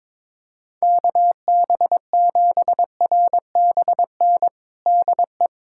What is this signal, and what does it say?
Morse code, heard as a keyed tone, about 22 wpm, 700 Hz. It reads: KB7RBN DE
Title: USA Callsign DE Callsign 22wpm